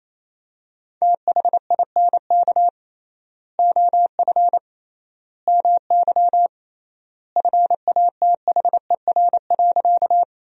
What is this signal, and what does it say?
Morse code, heard as a keyed tone, about 28 wpm, 700 Hz. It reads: THINK OF MY FATHER.
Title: 3x Word-Spacing - Sentences from Top 300 Words - 28wpm